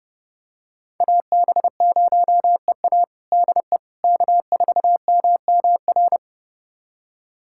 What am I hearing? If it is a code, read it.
Morse code, heard as a keyed tone, about 32 wpm, 700 Hz. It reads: AB0EA DE K4MMR
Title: USA Callsign DE Callsign - 5 Characters - 30wpm